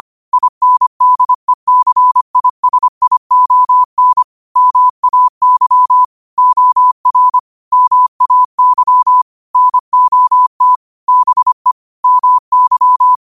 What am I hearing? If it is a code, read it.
Morse code, heard as a keyed tone, about 25 wpm, 1000 Hz. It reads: INDECISION MAY OR MAY NOT BE MY
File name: qotdfiles / 25wpm